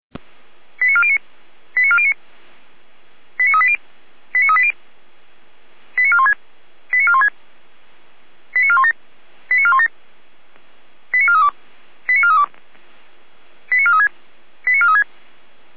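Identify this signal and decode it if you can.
Five-tone selective call (ZVEI), heard as quick runs of five tones, tones of about 70 ms. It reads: 89409 89409 89380 89380 89416 89416 89417 89417 89432 89432 89436 89436
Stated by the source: ZVEI five-tone selective call received on an SDR.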